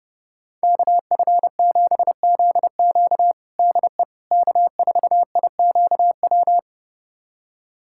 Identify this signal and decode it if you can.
Morse code, heard as a keyed tone, about 32 wpm, 700 Hz. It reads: KF7ZQ DE K4IQW
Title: USA Callsign DE Callsign - 5 Characters - 30wpm